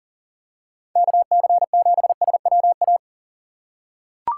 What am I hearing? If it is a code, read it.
Morse code, heard as a keyed tone, about 40 wpm, 700 Hz. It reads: KC7SWA
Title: USA Call Signs - 6 Characters - 6x Character Spacing - 40wpm